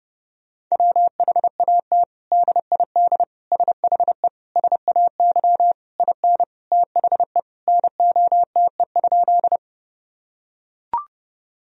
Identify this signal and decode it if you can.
Morse code, heard as a keyed tone, about 30 wpm, 700 Hz. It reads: WHAT DID SHE SAY IN THE NOTE?
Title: Sentences from Top 1000 Words - Review Part 2 - 30wpm